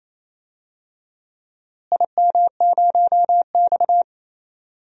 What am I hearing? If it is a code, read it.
Morse code, heard as a keyed tone, about 28 wpm, 700 Hz. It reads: IM0X
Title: DXCC-100 Call Signs - 4 Characters - Repeated 3x - 28wpm